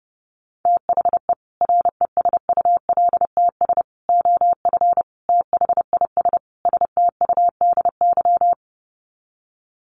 Morse code, heard as a keyed tone, about 30 wpm, 700 Hz. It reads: THE RESULTS OF THIS STUDY
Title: Sets of 5 Words 30wpm